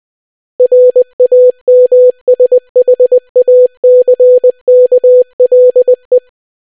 Morse code, heard as a keyed tone, about 20 wpm, 500 Hz. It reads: RAMSHACKLE